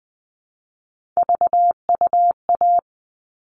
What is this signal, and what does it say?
Morse code, heard as a keyed tone, about 20 wpm, 700 Hz. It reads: VUA
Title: All Cut Numbers - 3-Digit 20wpm